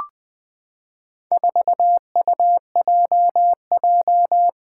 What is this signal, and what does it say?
Morse code, heard as a keyed tone, about 20 wpm, 700 Hz. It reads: 4UJJ
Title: Four Challenging Letter-Number Combinations 20wpm